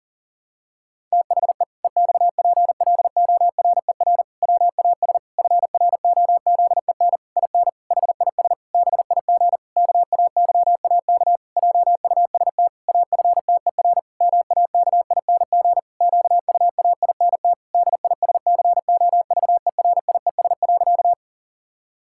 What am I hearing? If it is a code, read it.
Morse code, heard as a keyed tone, about 40 wpm, 700 Hz. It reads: THE EXPLORER WAS FROZEN IN HIS BIG KAYAK JUST AFTER MAKING QUAINT DISCOVERIES.
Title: Pangram Challenge v2 - 40wpm